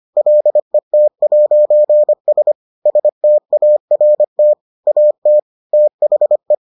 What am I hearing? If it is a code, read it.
Morse code, heard as a keyed tone, about 25 wpm, 600 Hz. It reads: LET'S START AT THE